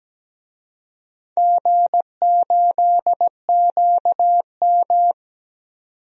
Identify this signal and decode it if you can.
Morse code, heard as a keyed tone, about 17 wpm, 700 Hz. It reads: G8QM
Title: DXCC-100 Call Signs - 4 Characters - 17wpm